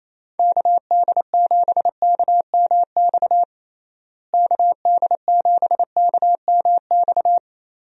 Morse code, heard as a keyed tone, about 28 wpm, 700 Hz. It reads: KD7KMX KD7KMX
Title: USA Call Signs - 6 Characters - Repeated 2x - 28wpm